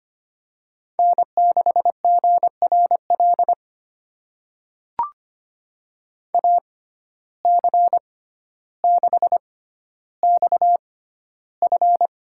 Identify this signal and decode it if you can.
Morse code, heard as a keyed tone, about 25 wpm, spaced as for 13 wpm, 700 Hz. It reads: N6GRL AC6XF
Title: USA Call Signs - 5 Characters - 6x Character Spacing - 25wpm